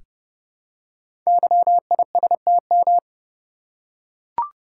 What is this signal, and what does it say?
Morse code, heard as a keyed tone, about 30 wpm, 700 Hz. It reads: YISTM
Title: Code Groups - 5 Letters - 30wpm